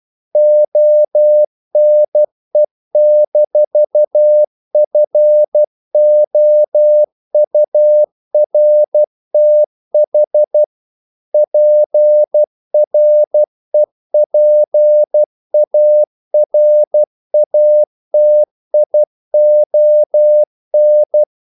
Morse code, heard as a keyed tone, about 12 wpm, 600 Hz. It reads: ONE-FOURTH PREPARATION
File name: CW / Practice / EnglishQuotations0312